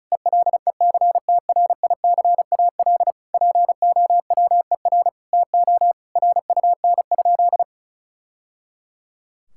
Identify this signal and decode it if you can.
Morse code, heard as a keyed tone, about 36 wpm, 700 Hz. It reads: ELECTRICAL POWER TO RUN?
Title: Sentences from Top 1000 Words - Review Part 6 - 35wpm